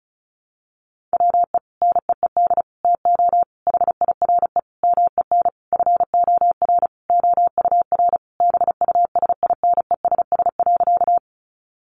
Morse code, heard as a keyed tone, about 34 wpm, 700 Hz. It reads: WE NEED TO HIRE MEN FOR OUR BUSINESS.
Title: Sentences from Top 1400 Words 35wpm